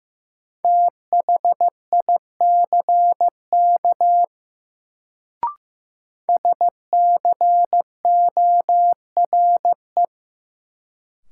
Morse code, heard as a keyed tone, about 15 wpm, 700 Hz. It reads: THICK SCORE
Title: Sentences from Top 1600 Words - New Word Drill - 15wpm